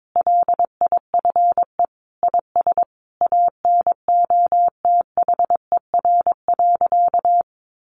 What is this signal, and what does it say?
Morse code, heard as a keyed tone, about 22 wpm, 700 Hz. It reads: LIFE IS ANOTHER.
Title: Sentences from Top 1000 Words 22wpm